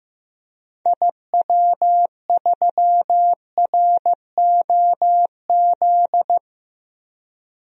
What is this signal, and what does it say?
Morse code, heard as a keyed tone, about 15 wpm, 700 Hz. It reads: IW3ROZ
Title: DXCC-100 Call Signs - 6 Characters - Repeated 2x - 15wpm